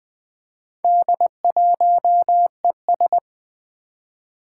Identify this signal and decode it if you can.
Morse code, heard as a keyed tone, about 20 wpm, 700 Hz. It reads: D1ES
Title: Four Challenging Letter-Number Combinations 20wpm